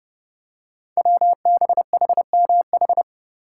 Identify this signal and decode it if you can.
Morse code, heard as a keyed tone, about 30 wpm, 700 Hz. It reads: WBHMH
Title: Code Groups - 5 Letter:Numbers - 30wpm